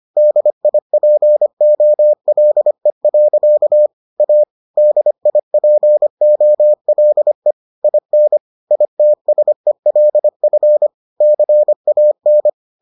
Morse code, heard as a keyed tone, about 25 wpm, 600 Hz. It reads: DIPOLE. A DIPOLE IN ITSELF CAN